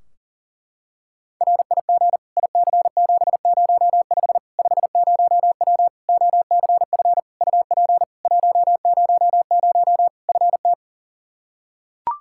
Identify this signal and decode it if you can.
Morse code, heard as a keyed tone, about 40 wpm, 700 Hz. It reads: RIG IC705 50W OCF UP 100 FT
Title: Practice QSO - Exchange 2 - Info - 40wpm